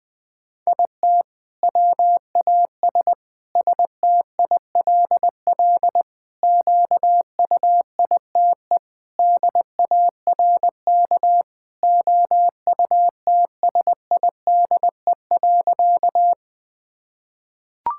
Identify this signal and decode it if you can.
Morse code, heard as a keyed tone, about 21 wpm, 700 Hz. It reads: IT WAS STILL QUITE DARK OUTSIDE.
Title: Sentences from Top 1000 Words - Review Part 6 - 20wpm